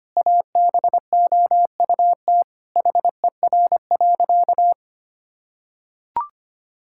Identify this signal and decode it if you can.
Morse code, heard as a keyed tone, about 25 wpm, 700 Hz. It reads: ABOUT HER.
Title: Sentences from Top 1600 Words 25wpm